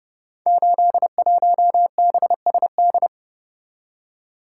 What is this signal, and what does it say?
Morse code, heard as a keyed tone, about 28 wpm, 700 Hz. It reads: 81BSD
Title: Code Groups - 5 Letter:Numbers - 30wpm